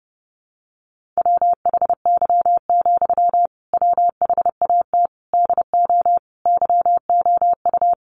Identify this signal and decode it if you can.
Morse code, heard as a keyed tone, about 30 wpm, 700 Hz. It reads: WHY, WHAT DO YOU
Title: Sentences from Top 200 Words 30wpm